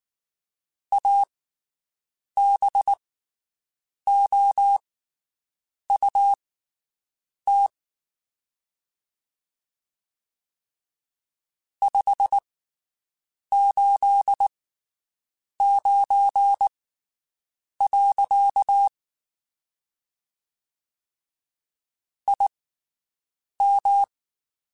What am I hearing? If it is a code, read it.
Morse code, heard as a keyed tone, about 19 wpm, spaced as for 7 wpm, 800 Hz. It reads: ABOUT 589. IM
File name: QSOs Spaced